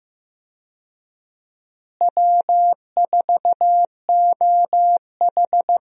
Morse code, heard as a keyed tone, about 15 wpm, 700 Hz. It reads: W4OH
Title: USA Call Signs - 4 Characters - Repeated 3x - 15wpm